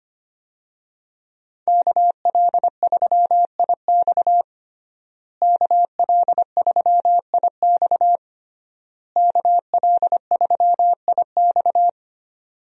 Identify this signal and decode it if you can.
Morse code, heard as a keyed tone, about 25 wpm, 700 Hz. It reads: KL3IX KL3IX KL3IX
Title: USA Call Signs - 5 Characters - Repeated 3x - 25wpm